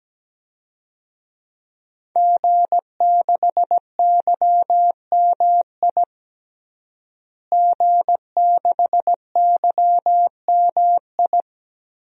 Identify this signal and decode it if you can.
Morse code, heard as a keyed tone, about 17 wpm, 700 Hz. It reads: G6YMI G6YMI
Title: DXCC-100 Call Signs - 5 Characters - Repeated 2x - 17wpm